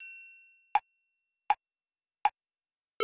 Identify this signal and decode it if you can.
There is no code.